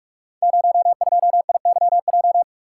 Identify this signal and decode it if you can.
Morse code, heard as a keyed tone, about 44 wpm, 700 Hz. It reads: 02IYJ
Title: Code Groups - 5 Letter:Numbers - 45wpm